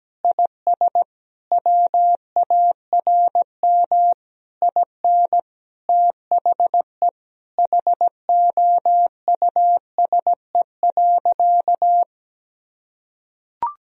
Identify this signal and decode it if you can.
Morse code, heard as a keyed tone, about 17 wpm, 700 Hz. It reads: IS WARM IN THE HOUSE.
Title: Sentences from Top 1000 Words - Review Part 4 - 17wpm